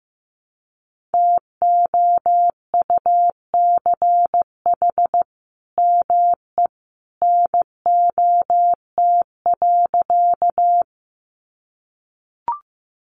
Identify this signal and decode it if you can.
Morse code, heard as a keyed tone, about 15 wpm, 700 Hz. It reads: TOUCH ME NOT.
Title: Sentences from Top 1200 Words 15wpm